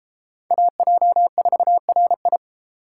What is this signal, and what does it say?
Morse code, heard as a keyed tone, about 32 wpm, 700 Hz. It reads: AJ4RI